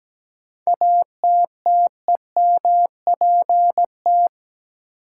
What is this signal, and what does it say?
Morse code, heard as a keyed tone, about 17 wpm, 700 Hz. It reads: ATTEMPT
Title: Sets of 4 Words - New Word Drill - 17wpm